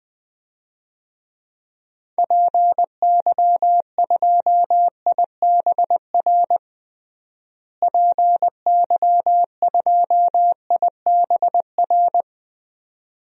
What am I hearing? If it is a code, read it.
Morse code, heard as a keyed tone, about 20 wpm, 700 Hz. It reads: PY2IBR PY2IBR
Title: DXCC-100 Call Signs - 6 Characters - Repeated 2x - 20wpm